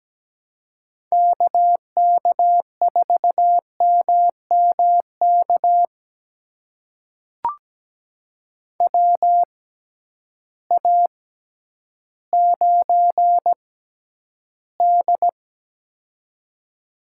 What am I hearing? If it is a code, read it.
Morse code, heard as a keyed tone, about 17 wpm, spaced as for 10 wpm, 700 Hz. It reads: KK4MMK WA9D
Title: USA Call Signs - 6x Character Spacing - 17wpm